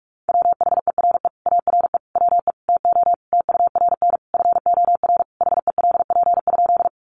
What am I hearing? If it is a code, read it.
Morse code, heard as a keyed tone, about 44 wpm, 700 Hz. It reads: WHERE ARE WE TO TURN FOR HELP?
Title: Sentences from Top 200 Words 45wpm